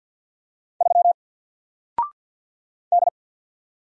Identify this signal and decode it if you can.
Morse code, heard as a keyed tone, about 48 wpm, 700 Hz. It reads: 3 D
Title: Single Letter-Number Alt 50wpm